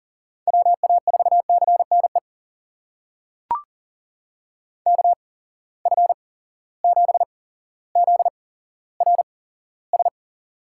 Morse code, heard as a keyed tone, about 40 wpm, spaced as for 17 wpm, 700 Hz. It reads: WA4CNE KF7ZRS
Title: USA Call Signs - 6 Characters - 8x Character Spacing - 40wpm